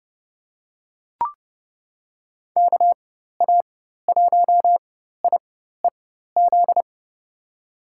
Morse code, heard as a keyed tone, about 30 wpm, spaced as for 14 wpm, 700 Hz. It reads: KA1IEZ
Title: USA Call Signs - 6 Characters - 4x Character Spacing - 30wpm